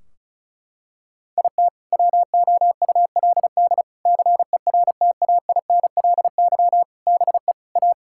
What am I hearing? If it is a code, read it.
Morse code, heard as a keyed tone, about 34 wpm, 700 Hz. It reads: IT WOULD CERTAINLY BE A
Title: Sentences from Top 1600 Words 35wpm